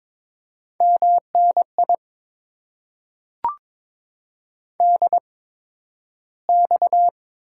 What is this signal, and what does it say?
Morse code, heard as a keyed tone, about 22 wpm, spaced as for 12 wpm, 690 Hz. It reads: MNI DX
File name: Top 90 QSO Elements - 8x Character Spacing - 22wpm